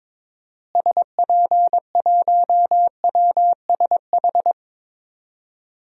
Morse code, heard as a keyed tone, about 22 wpm, 700 Hz. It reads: SP1WSH
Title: DXCC-100 Call Signs - 6 Characters - Repeated 3x - 22wpm